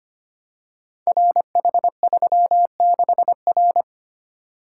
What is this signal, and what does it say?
Morse code, heard as a keyed tone, about 25 wpm, 700 Hz. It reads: RH36R
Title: Code Groups - 5 Letter:Numbers - 25wpm